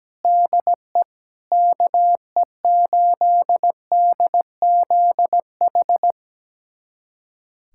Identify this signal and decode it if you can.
Morse code, heard as a keyed tone, about 17 wpm, 700 Hz. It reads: DE KE8DZH
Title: USA Callsign DE Callsign 17wpm